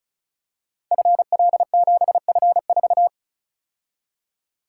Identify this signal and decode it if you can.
Morse code, heard as a keyed tone, about 36 wpm, 700 Hz. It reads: FL7F4
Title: Code Groups - 5 Letter:Numbers - 35wpm